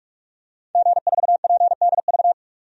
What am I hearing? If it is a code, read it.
Morse code, heard as a keyed tone, about 46 wpm, 700 Hz. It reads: G4PDV
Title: DXCC-100 Call Signs - 5 Characters - 45wpm